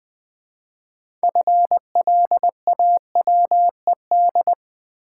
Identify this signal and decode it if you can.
Morse code, heard as a keyed tone, about 20 wpm, 700 Hz. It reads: FLAWED
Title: 6 Letter Words - First 3 Letters Given - 20wpm